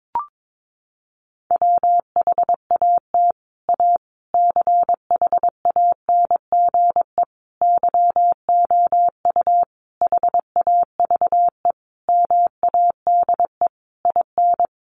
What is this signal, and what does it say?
Morse code, heard as a keyed tone, about 22 wpm, 700 Hz. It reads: WHAT A CHANGE YOU HAVE MADE IN
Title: Sentences from Top 600 Words 22wpm